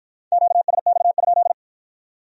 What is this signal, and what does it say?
Morse code, heard as a keyed tone, about 53 wpm, 700 Hz. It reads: QSX?